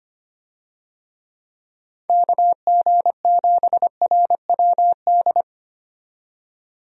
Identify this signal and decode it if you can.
Morse code, heard as a keyed tone, about 25 wpm, 700 Hz. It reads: KG7RWD
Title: USA Call Signs - 6 Characters - 25wpm